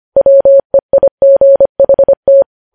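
Morse code, heard as a keyed tone, about 25 wpm, 550 Hz. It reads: WEIGHT
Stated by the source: Morse news headlines sent at 25 wpm.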